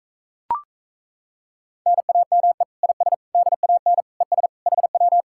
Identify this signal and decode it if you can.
Morse code, heard as a keyed tone, about 38 wpm, 700 Hz. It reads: NAME IS DAN ES HW